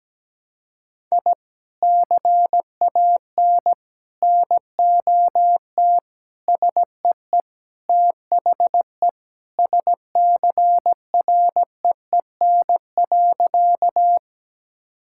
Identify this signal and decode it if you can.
Morse code, heard as a keyed tone, about 17 wpm, 700 Hz. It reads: I CAN NOT SEE THE SCREEN.
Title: Sentences from Top 1000 Words - Review Part 6 - 17wpm